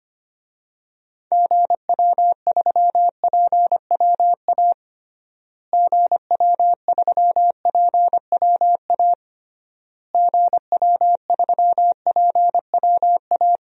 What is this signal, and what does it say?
Morse code, heard as a keyed tone, about 25 wpm, 700 Hz. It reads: GW3PWA GW3PWA GW3PWA
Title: DXCC-100 Call Signs - 6 Characters - Repeated 3x - 25wpm